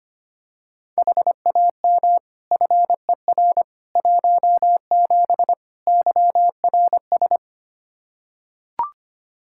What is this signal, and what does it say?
Morse code, heard as a keyed tone, about 25 wpm, 700 Hz. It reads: HAM FER 17 YRS
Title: Practice QSO - Exchange 3 - More Info - 25wpm